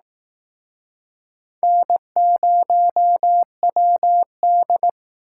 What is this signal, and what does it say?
Morse code, heard as a keyed tone, about 18 wpm, 700 Hz. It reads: N0WD
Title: Speed Racing - USA Call Signs 15wpm